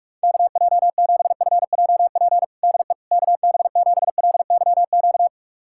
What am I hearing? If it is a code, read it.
Morse code, heard as a keyed tone, about 46 wpm, 700 Hz. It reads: KJ7FJP DE KB7LYQ
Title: USA Callsign DE Callsign - 6 Characters - 45wpm